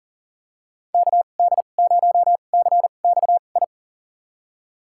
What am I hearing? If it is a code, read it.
Morse code, heard as a keyed tone, about 40 wpm, spaced as for 29 wpm, 700 Hz. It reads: KD0CXI
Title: USA Call Signs - 6 Characters - 2x Character Spacing - 40wpm